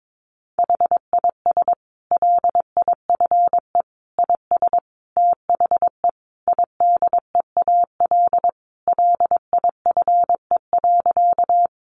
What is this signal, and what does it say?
Morse code, heard as a keyed tone, about 22 wpm, 700 Hz. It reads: HIS LIFE IS THE IDEAL LIFE.
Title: Sentences from Top 1400 Words 22wpm